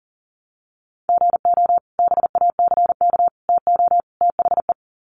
Morse code, heard as a keyed tone, about 40 wpm, 700 Hz. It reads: GO BACK TO THE
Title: Sets of 4 Words 40wpm